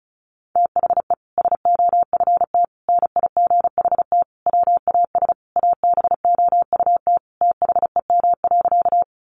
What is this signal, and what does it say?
Morse code, heard as a keyed tone, about 34 wpm, 700 Hz. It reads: THE SOFT NIGHT WAS ABOUT THEM.
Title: Sentences from Top 1500 Words 35wpm